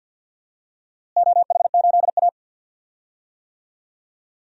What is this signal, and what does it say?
Morse code, heard as a keyed tone, about 47 wpm, 700 Hz. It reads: OH8A
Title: DXCC-100 Call Signs - 4 Characters - Repeated 2x - 50wpm